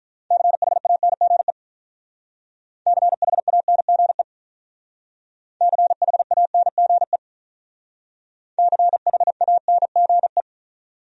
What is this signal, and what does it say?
Morse code, heard as a keyed tone, about 45 wpm, 700 Hz. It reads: CHANGE CHANGE CHANGE CHANGE